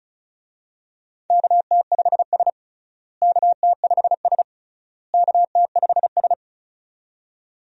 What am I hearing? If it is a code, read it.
Morse code, heard as a keyed tone, about 34 wpm, 700 Hz. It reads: KT5S KT5S KT5S
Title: USA Call Signs - 4 Characters - Repeated 3x - 35wpm